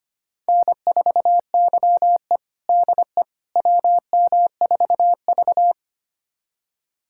Morse code, heard as a keyed tone, about 25 wpm, 700 Hz. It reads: N4YE DE WM4V
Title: USA Callsign DE Callsign - 4 Characters - 25wpm